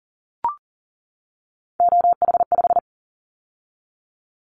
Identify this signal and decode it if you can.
Morse code, heard as a keyed tone, about 40 wpm, 700 Hz. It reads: OH5
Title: Callsign Prefixes for Top 25 DXCC - 40wpm